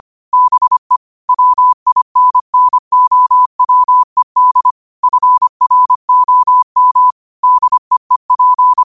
Morse code, heard as a keyed tone, about 25 wpm, 1000 Hz. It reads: BE WINNOWED FROM DEEP